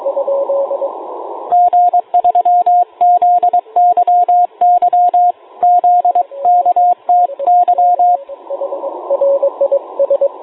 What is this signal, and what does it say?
Morse code, heard as a keyed tone, about 23 wpm, 710 Hz, with radio noise. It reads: G3ZYY ZKTY